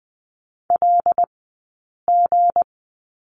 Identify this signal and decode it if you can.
Morse code, heard as a keyed tone, about 20 wpm, 700 Hz. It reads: L G
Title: 190 Lesson 31d - Letter v - Two Characters 20wpm